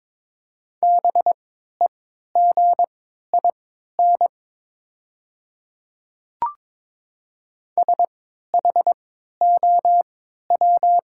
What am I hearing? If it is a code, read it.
Morse code, heard as a keyed tone, about 22 wpm, spaced as for 13 wpm, 700 Hz. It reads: BEGIN SHOW
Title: 3x Character Spacing - Top 300 Words - 22wpm